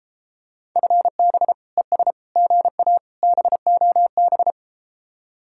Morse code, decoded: FB ES GA BOB

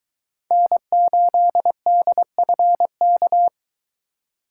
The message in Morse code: N8DFK

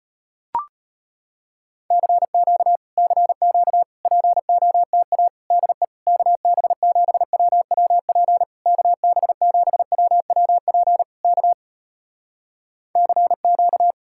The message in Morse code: CQ CQ POTA DE KB7WWP KB7WWP K CQ

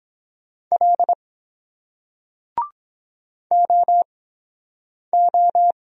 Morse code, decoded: L OO

